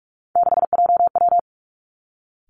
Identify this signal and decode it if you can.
Morse code, heard as a keyed tone, about 46 wpm, 700 Hz. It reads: 6JW
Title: Three Challenging Letter-Number Combinations 45wpm